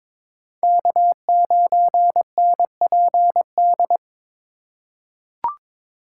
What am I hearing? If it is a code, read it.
Morse code, heard as a keyed tone, about 21 wpm, 700 Hz. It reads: K9NPD
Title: USA Call Signs - 4x Character Spacing - 22wpm